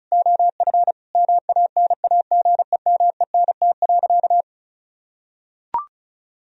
Morse code, heard as a keyed tone, about 36 wpm, 700 Hz. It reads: OF MANAGEMENT.
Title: Sentences from Top 1000 Words - Review Part 4 - 35wpm